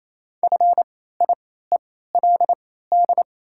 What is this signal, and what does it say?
Morse code, heard as a keyed tone, about 28 wpm, spaced as for 16 wpm, 700 Hz. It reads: FIELD
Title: 3x Character Spacing - Top 500 Words - 28wpm